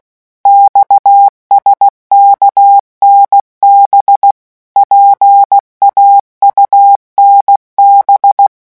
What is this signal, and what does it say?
Morse code, heard as a keyed tone, about 16 wpm, 800 Hz. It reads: XSKNB PAUNB
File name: morse mp3 / revision5-vitesse 15 mots